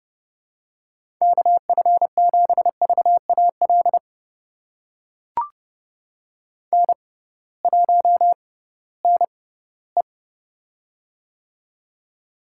Morse code, heard as a keyed tone, about 32 wpm, spaced as for 18 wpm, 700 Hz. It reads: KF7VAL N1NE
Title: USA Call Signs - 6x Character Spacing - 30wpm